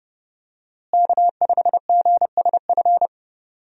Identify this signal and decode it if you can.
Morse code, heard as a keyed tone, about 32 wpm, 700 Hz. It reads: K5GSF
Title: USA Call Signs - 5 Characters - Repeated 3x - 30wpm